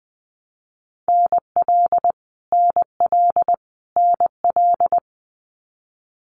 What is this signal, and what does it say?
Morse code, heard as a keyed tone, about 20 wpm, 700 Hz. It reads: NL NL NL
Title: Canadian Provinces and Territories - Repeated 3x - 20wpm